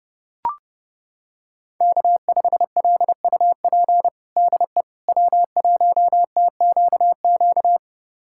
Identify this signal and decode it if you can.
Morse code, heard as a keyed tone, about 32 wpm, 700 Hz. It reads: K5LUP DE W1TQQ
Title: USA Callsign DE Callsign - 5 Characters - 30wpm